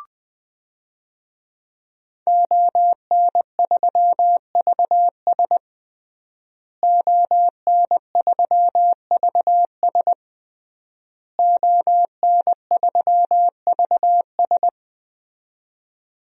Morse code, heard as a keyed tone, about 21 wpm, 700 Hz. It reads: ON3VS ON3VS ON3VS
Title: DXCC-100 Call Signs - 5 Characters - Repeated 3x - 20wpm